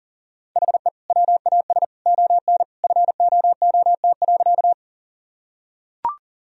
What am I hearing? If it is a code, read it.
Morse code, heard as a keyed tone, about 40 wpm, 700 Hz. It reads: HE WAS ON FOOT.
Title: Sentences from Top 1000 Words - Review Part 3 - 40wpm